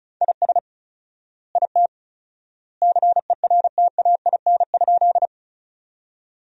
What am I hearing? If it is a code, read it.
Morse code, heard as a keyed tone, about 34 wpm, 700 Hz. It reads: IS IT CERTAIN?
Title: 4x Word-Spacing - Sentences from Top 500 Words - 35wpm